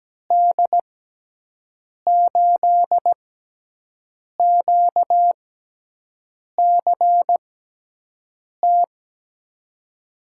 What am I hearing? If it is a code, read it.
Morse code, heard as a keyed tone, about 17 wpm, spaced as for 6 wpm, 700 Hz. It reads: D8QCT
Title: USA Call Signs - 6 Characters - 6x Character Spacing - 17wpm